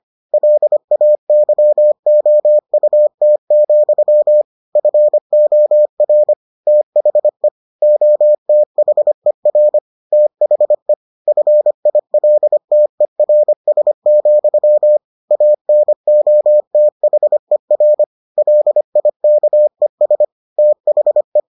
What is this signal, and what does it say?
Morse code, heard as a keyed tone, about 25 wpm, 600 Hz. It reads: LAYOUT, FOR THE OTHER THE FILTERS, ANOTHER LIKES THE